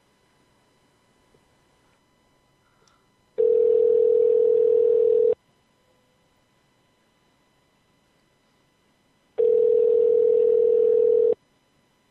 Ringback tone (North America), 440 + 480 Hz.